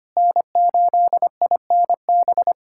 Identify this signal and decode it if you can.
Morse code, heard as a keyed tone, about 25 wpm, 700 Hz. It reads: N8INB
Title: USA Callsign DE Callsign - 5 Characters - 25wpm